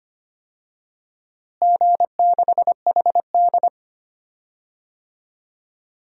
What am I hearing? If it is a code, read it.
Morse code, heard as a keyed tone, about 25 wpm, 700 Hz. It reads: G6HD
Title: DXCC-100 Call Signs - 4 Characters - Repeated 2x - 25wpm